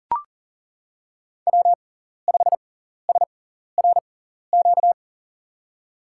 Morse code, heard as a keyed tone, about 40 wpm, spaced as for 14 wpm, 700 Hz. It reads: W5SRQ
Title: USA Call Signs - 5 Characters - 6x Character Spacing - 40wpm